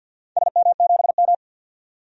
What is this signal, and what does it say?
Morse code, heard as a keyed tone, about 49 wpm, 700 Hz. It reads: SM7M
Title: DXCC-100 Call Signs - 4 Characters - 50wpm